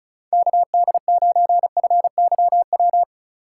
Morse code, heard as a keyed tone, about 34 wpm, 700 Hz. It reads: KD9FYW
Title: USA Call Signs - 6 Characters - 8x Character Spacing - 35wpm